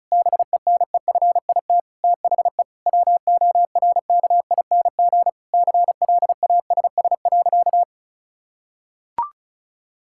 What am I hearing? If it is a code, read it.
Morse code, heard as a keyed tone, about 36 wpm, 700 Hz. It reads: BENEFIT THE WORKING CLASS.
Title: Sentences from Top 1000 Words - Review Part 6 - 35wpm